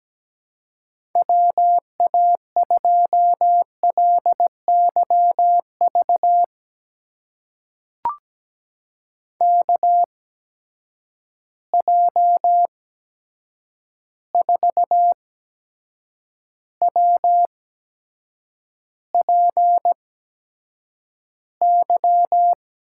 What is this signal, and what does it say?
Morse code, heard as a keyed tone, about 17 wpm, spaced as for 7 wpm, 700 Hz. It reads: WA2LYV KJ4WPY
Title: USA Call Signs - 8x Character Spacing - 17wpm